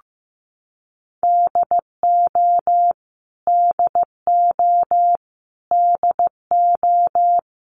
Morse code, heard as a keyed tone, about 15 wpm, 700 Hz. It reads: DO DO DO
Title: Top 500 Words Repeated 15wpm